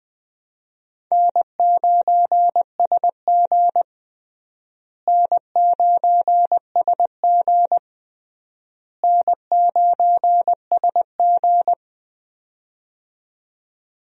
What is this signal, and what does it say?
Morse code, heard as a keyed tone, about 20 wpm, 700 Hz. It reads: N9SG N9SG N9SG
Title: USA Call Signs - 4 Characters - Repeated 3x - 20wpm